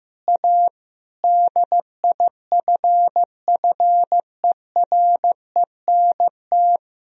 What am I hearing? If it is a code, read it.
Morse code, heard as a keyed tone, about 15 wpm, 700 Hz. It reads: A DIFFERENT
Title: Sentences from Top 1000 Words - Review Part 3 - 15wpm